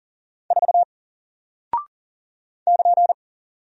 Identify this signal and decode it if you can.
Morse code, heard as a keyed tone, about 40 wpm, 700 Hz. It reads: 4 <KN>
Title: Single Character (Letter:Number:Punctuation:Prosign) Alt 40wpm